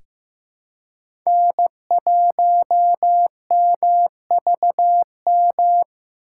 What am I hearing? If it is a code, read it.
Morse code, heard as a keyed tone, about 15 wpm, 700 Hz. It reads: N1MVM
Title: USA Call Signs - 5 Characters - 2x Character Spacing - 15wpm